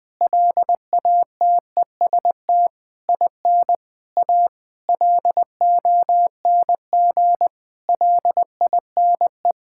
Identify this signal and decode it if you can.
Morse code, heard as a keyed tone, about 20 wpm, 700 Hz. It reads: LATEST IN A LONG LINE
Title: Sentences with 7-Word Phrases - 20wpm